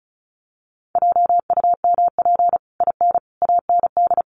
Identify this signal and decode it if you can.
Morse code, heard as a keyed tone, about 34 wpm, 700 Hz. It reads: JUMP IN AND
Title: Sentences from Top 1300 Words 35wpm